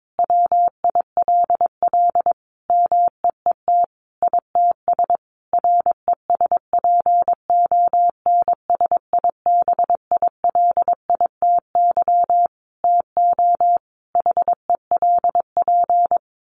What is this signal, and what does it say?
Morse code, heard as a keyed tone, about 22 wpm, 700 Hz. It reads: WILL MEET ITS RESPONSIBILITY TO HELP
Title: Sentences from Top 1200 Words 22wpm